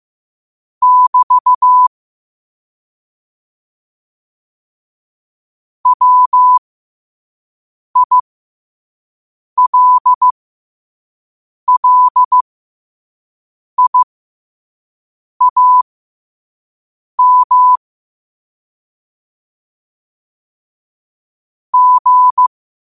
Morse code, heard as a keyed tone, about 15 wpm, spaced as for 5 wpm, 1000 Hz. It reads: <BT> WILLIAM G
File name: 2026-02-15-5wpm